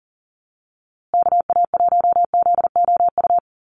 Morse code, heard as a keyed tone, about 40 wpm, 700 Hz. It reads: KA1ZOU
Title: USA Call Signs Repeated 2x 40wpm